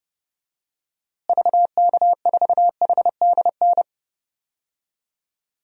Morse code, heard as a keyed tone, about 32 wpm, 700 Hz. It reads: VK4HDN
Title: DXCC-100 Call Signs - 6 Characters - Repeated 3x - 30wpm